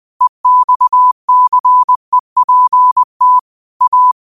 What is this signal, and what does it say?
Morse code, heard as a keyed tone, about 20 wpm, 1000 Hz. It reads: EXCEPT A
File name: qotdfiles / 20wpm